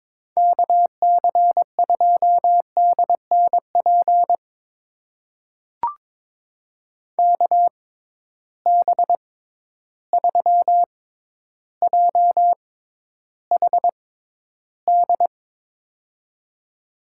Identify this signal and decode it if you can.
Morse code, heard as a keyed tone, about 22 wpm, spaced as for 11 wpm, 700 Hz. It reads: KC2DNP KB3JHD
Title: USA Call Signs - 6 Characters - 6x Character Spacing - 22wpm